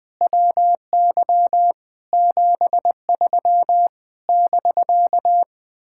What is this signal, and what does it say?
Morse code, heard as a keyed tone, about 20 wpm, 700 Hz. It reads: WY 73 <BK>
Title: Speed-Racing Common QSO Phrases Encore - 17wpm